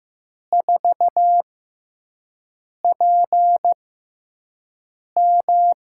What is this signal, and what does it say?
Morse code, heard as a keyed tone, about 15 wpm, spaced as for 5 wpm, 700 Hz. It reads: 4PM
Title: USA Call Signs - 5 Characters - 6x Character Spacing - 15wpm